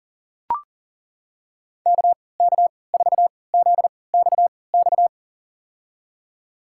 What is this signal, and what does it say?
Morse code, heard as a keyed tone, about 40 wpm, spaced as for 23 wpm, 700 Hz. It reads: KK4ZXX
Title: USA Call Signs - 6 Characters - 3x Character Spacing - 40wpm